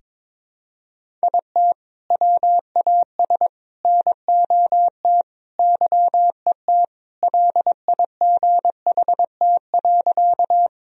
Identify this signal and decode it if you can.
Morse code, heard as a keyed tone, about 22 wpm, 700 Hz. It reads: IT WAS NOT YET LIGHT.
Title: Sentences from Top 1000 Words - Review Part 1 - 22wpm